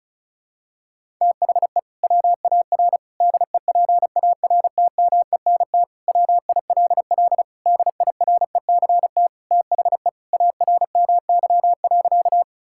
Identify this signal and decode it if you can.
Morse code, heard as a keyed tone, about 36 wpm, 700 Hz. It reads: THE WAR DEPARTMENT WILL DIRECT THE ARMY.
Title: Sentences from Top 1000 Words - Review Part 5 - 35wpm